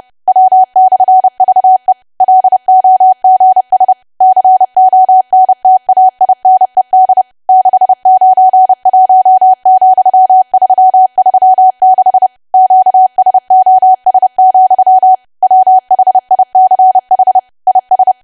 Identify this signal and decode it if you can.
Morse code, heard as a keyed tone, about 30 wpm, 750 Hz. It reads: W/VE LOGS CONTAINED 691,336 QSOS, WHICH IS
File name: cwa / W1AW-Level3 / 30WPM